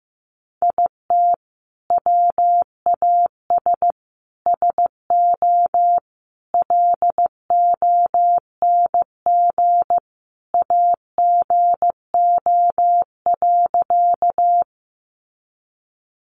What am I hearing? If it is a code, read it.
Morse code, heard as a keyed tone, about 15 wpm, 700 Hz. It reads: IT WAS SO LONG AGO.
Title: Sentences from Top 500 Words 15wpm